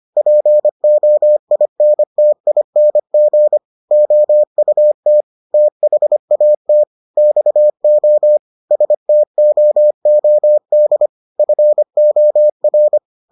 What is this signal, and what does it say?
Morse code, heard as a keyed tone, about 25 wpm, 600 Hz. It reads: POINTING OUT THAT XO STOOD FOR